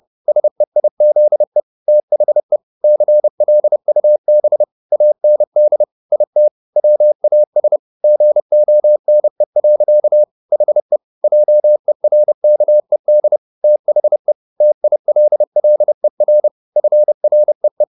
Morse code, heard as a keyed tone, about 30 wpm, 600 Hz. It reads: SEIZE THE CLUB AND IT WAS GONE. HE JERKED THE TILLER FREE